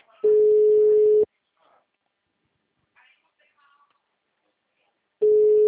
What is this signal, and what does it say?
Ringback tone (Europe, 425 Hz), with speech.